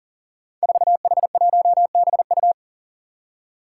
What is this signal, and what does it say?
Morse code, heard as a keyed tone, about 40 wpm, 700 Hz. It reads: 4H1BU
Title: Five Challenging Letter-Number Combinations 40wpm